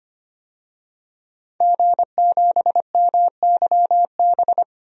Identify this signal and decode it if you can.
Morse code, heard as a keyed tone, about 25 wpm, 700 Hz. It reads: G7MYB
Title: DXCC-100 Call Signs - 5 Characters - 25wpm